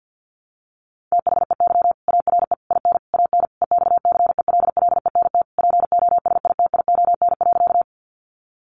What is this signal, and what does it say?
Morse code, heard as a keyed tone, about 48 wpm, 700 Hz. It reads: THEY ARE IN AN EXCELLENT POSITION.
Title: Sentences from Top 1000 Words 50wpm